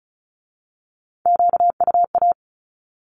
Morse code, heard as a keyed tone, about 36 wpm, 700 Hz. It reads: QUA
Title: Q-Codes with Spoken Meaning - 35wpm